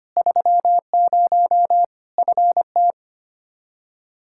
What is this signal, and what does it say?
Morse code, heard as a keyed tone, about 25 wpm, 700 Hz. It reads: 30 FT
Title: Practice QSO - Exchange 2 - Info - 25wpm